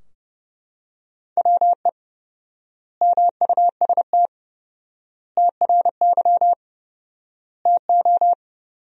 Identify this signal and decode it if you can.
Morse code, heard as a keyed tone, about 30 wpm, 700 Hz. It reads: WE MUST TRY TO
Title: 4x Word-Spacing - Sentences from Top 400 Words - 30wpm